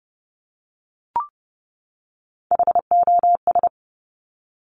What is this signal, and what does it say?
Morse code, heard as a keyed tone, about 30 wpm, 700 Hz. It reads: HOS
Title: Top Three Letter Combinations in Words 30wpm